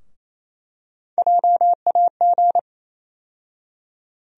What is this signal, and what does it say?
Morse code, heard as a keyed tone, about 28 wpm, 700 Hz. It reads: JAG
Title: USA Call Sign Postfixes - 3 Letters 28wpm